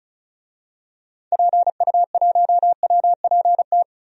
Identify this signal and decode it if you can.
Morse code, heard as a keyed tone, about 36 wpm, 700 Hz. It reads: PU1WPT
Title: DXCC-100 Call Signs - 6 Characters - 35wpm